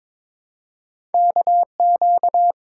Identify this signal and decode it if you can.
Morse code, heard as a keyed tone, about 23 wpm, 700 Hz. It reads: KQ
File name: practice / 22wpm / Two Letter-Number Combinations - 22wpm